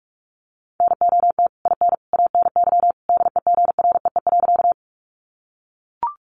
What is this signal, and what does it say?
Morse code, heard as a keyed tone, about 44 wpm, 700 Hz. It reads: NOT IN ANY DEGREE.
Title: Sentences from Top 1500 Words 45wpm